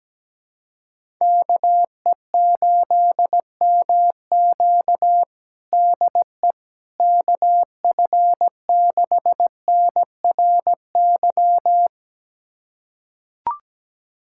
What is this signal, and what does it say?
Morse code, heard as a keyed tone, about 17 wpm, 700 Hz. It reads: KE8MQ DE KF6NRY
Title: USA Callsign DE Callsign 17wpm